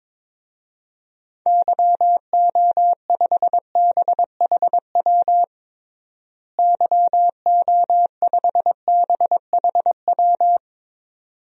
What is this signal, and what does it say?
Morse code, heard as a keyed tone, about 22 wpm, 700 Hz. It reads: YO5BHW YO5BHW
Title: DXCC-100 Call Signs - 6 Characters - Repeated 2x - 22wpm